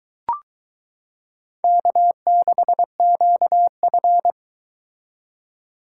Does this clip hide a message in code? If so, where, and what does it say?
Yes: Morse code from 1.6 to 4.3 s, heard as a keyed tone, about 23 wpm, 700 Hz. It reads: K6QF